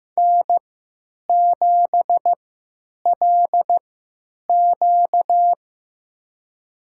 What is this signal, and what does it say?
Morse code, heard as a keyed tone, about 15 wpm, spaced as for 9 wpm, 700 Hz. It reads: N7LQ
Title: USA Call Signs - 4 Characters - 3x Character Spacing - 15wpm